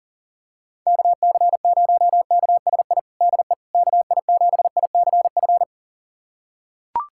Morse code, heard as a keyed tone, about 40 wpm, 700 Hz. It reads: KC0KSI DE KI7ICF
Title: USA Callsign DE Callsign - 6 Characters - 40wpm